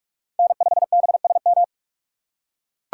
Morse code, heard as a keyed tone, about 44 wpm, 700 Hz. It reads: N5BSM